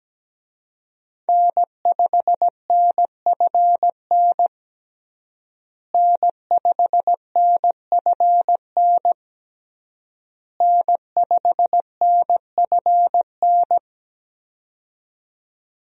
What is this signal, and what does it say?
Morse code, heard as a keyed tone, about 17 wpm, 700 Hz. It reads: N5NFN N5NFN N5NFN
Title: USA Call Signs - 5 Characters - Repeated 3x - 17wpm